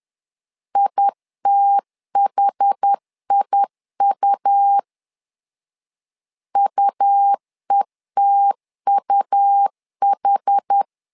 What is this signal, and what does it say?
Morse code, heard as a keyed tone, about 10 wpm, 800 Hz. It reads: ITHIU UETUH
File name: LEZIONI / 2 UHDN / 2 UH 10wpm 1714